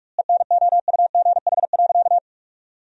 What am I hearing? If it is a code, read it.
Morse code, heard as a keyed tone, about 42 wpm, 700 Hz. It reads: ENOUGH.
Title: 2x Word-Spacing - Sentences from Top 400 Words - 45wpm